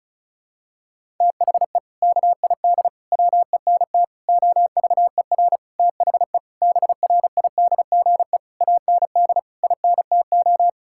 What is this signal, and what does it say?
Morse code, heard as a keyed tone, about 36 wpm, 700 Hz. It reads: THE KID WENT OVER THE BRIDGE AND INTO
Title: Sentences from Top 1600 Words 35wpm